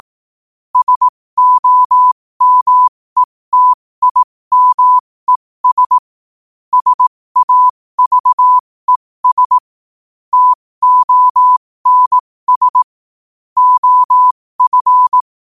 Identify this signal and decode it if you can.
Morse code, heard as a keyed tone, about 18 wpm, spaced as for 15 wpm, 1000 Hz. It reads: SOMETIMES SAVES TONS OF
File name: qotdfiles / 15wpm